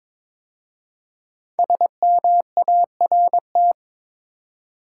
Morse code, heard as a keyed tone, about 22 wpm, 700 Hz. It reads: SMART
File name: 5 Letter Words - First 3 Letters Given - 22wpm